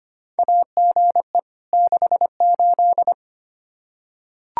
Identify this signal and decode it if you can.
Morse code, heard as a keyed tone, about 25 wpm, 700 Hz. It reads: AGE 68
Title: Practice QSO - Exchange 3 - More Info - 25wpm